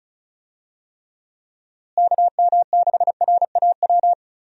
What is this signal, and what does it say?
Morse code, heard as a keyed tone, about 34 wpm, 700 Hz. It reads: KM6RAW
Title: USA Call Signs - 6 Characters - Repeated 3x - 35wpm